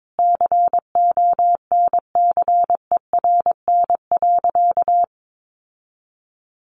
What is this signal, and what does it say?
Morse code, heard as a keyed tone, about 22 wpm, 700 Hz. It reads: CONCERN.
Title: Sentences from Top 1400 Words 22wpm